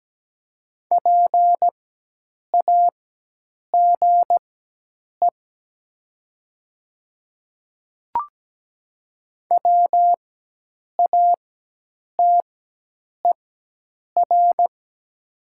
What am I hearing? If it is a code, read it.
Morse code, heard as a keyed tone, about 17 wpm, spaced as for 8 wpm, 700 Hz. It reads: PAGE WATER
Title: 4x Character Spacing - Top 500 Words - 17wpm